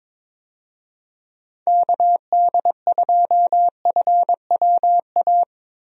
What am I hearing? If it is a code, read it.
Morse code, heard as a keyed tone, about 22 wpm, 700 Hz. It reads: KD2FWA
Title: USA Call Signs - 6 Characters - 22wpm